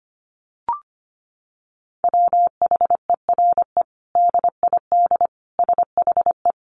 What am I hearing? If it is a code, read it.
Morse code, heard as a keyed tone, about 25 wpm, 700 Hz. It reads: WHERE DID SHE
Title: Sentences from Top 200 Words 25wpm